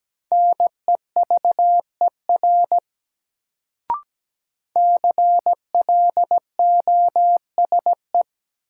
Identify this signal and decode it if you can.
Morse code, heard as a keyed tone, about 17 wpm, 700 Hz. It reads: NEVER CLOSE
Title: Sentences from Top 300 Words - New Word Drill - 17wpm